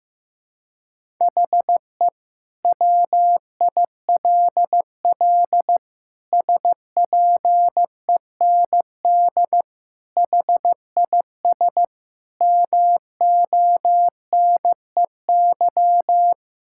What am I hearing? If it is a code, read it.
Morse code, heard as a keyed tone, about 16 wpm, 700 Hz. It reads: HE WILL SPEND HIS MONEY
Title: Sentences from Top 1000 Words - Review Part 5 - 15wpm